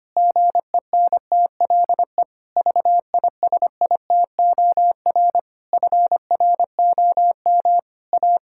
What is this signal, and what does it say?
Morse code, heard as a keyed tone, about 25 wpm, 700 Hz. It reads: GENTLE VISITOR FROM A
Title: Pangram Challenge v2 - 25wpm